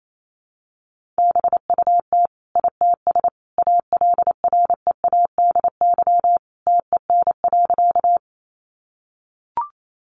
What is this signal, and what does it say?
Morse code, heard as a keyed tone, about 28 wpm, 700 Hz. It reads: BUT ITS ALREADY TEN.
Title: Sentences from Top 700 Words 28wpm